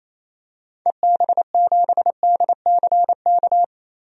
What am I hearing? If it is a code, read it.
Morse code, heard as a keyed tone, about 28 wpm, 700 Hz. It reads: EB7DCK
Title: DXCC-100 Call Signs - 6 Characters - 28wpm